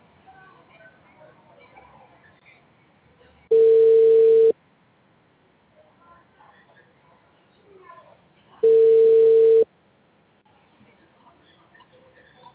Ringback tone (China, 450 Hz).